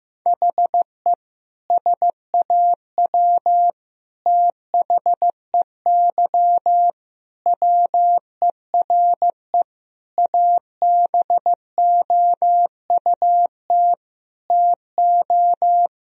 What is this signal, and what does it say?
Morse code, heard as a keyed tone, about 15 wpm, 700 Hz. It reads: HE SAW THEY WERE ABOUT TO
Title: Sentences from Top 1000 Words - Review Part 4 - 15wpm